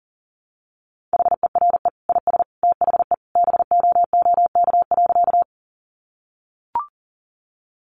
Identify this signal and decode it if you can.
Morse code, heard as a keyed tone, about 40 wpm, 700 Hz. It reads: HERE IS THE BOOK.
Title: Sentences from Top 300 Words 40wpm